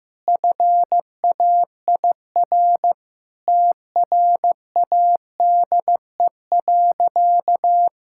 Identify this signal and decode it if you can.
Morse code, heard as a keyed tone, about 15 wpm, 700 Hz. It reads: FAIR TRADE.
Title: Sentences from Top 1600 Words 15wpm